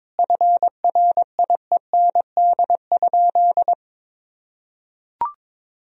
Morse code, heard as a keyed tone, about 21 wpm, 700 Hz. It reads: FRIEND?
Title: Sentences from Top 1000 Words - Review Part 3 - 22wpm